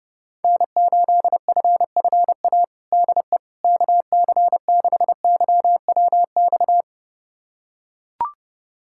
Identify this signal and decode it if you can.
Morse code, heard as a keyed tone, about 30 wpm, 700 Hz. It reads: N8FFA DE KC6YWX
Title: USA Callsign DE Callsign 30wpm